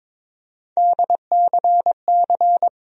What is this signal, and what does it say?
Morse code, heard as a keyed tone, about 22 wpm, 700 Hz. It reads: DCC
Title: USA Call Sign Postfixes - 3 Letters 22wpm